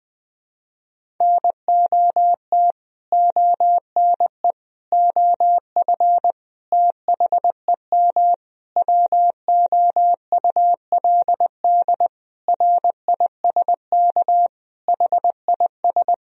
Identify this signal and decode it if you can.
Morse code, heard as a keyed tone, about 20 wpm, 700 Hz. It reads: NOT ONE OF THEM WOULD RISK HIS
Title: Sentences from Top 1000 Words - Review Part 4 - 20wpm